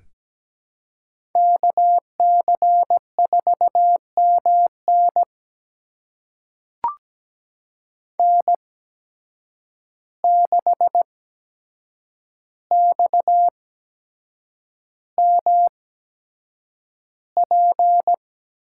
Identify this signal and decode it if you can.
Morse code, heard as a keyed tone, about 17 wpm, spaced as for 7 wpm, 700 Hz. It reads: KC4MN N6XMP
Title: USA Call Signs - 8x Character Spacing - 17wpm